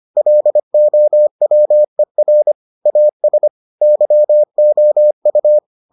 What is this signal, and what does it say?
Morse code, heard as a keyed tone, about 25 wpm, 600 Hz. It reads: LOWER AS YOU